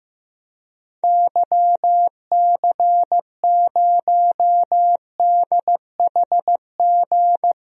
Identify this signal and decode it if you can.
Morse code, heard as a keyed tone, about 15 wpm, 700 Hz. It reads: YC0DHG
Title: DXCC-100 Call Signs - 6 Characters - 15wpm